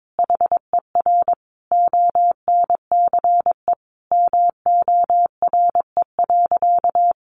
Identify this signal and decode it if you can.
Morse code, heard as a keyed tone, about 22 wpm, 700 Hz. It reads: HER ONCE MORE.
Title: Sentences from Top 500 Words 22wpm